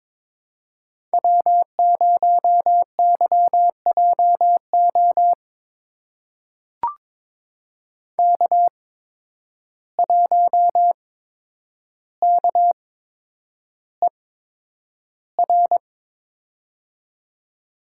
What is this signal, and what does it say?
Morse code, heard as a keyed tone, about 23 wpm, spaced as for 10 wpm, 700 Hz. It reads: W0YJO K1KER
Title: USA Call Signs - 5 Characters - 8x Character Spacing - 22wpm